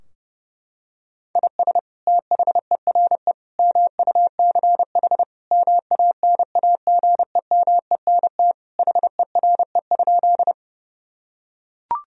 Morse code, heard as a keyed tone, about 32 wpm, 700 Hz. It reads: IS THERE MUCH MANAGEMENT HERE?
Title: Sentences from Top 1000 Words - Review Part 4 - 30wpm